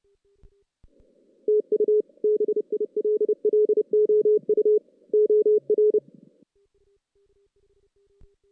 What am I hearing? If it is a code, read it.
Morse code, heard as a keyed tone, about 28 wpm, 410 Hz. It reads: TU BILLOU OR
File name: 2016 JAN CW NAQP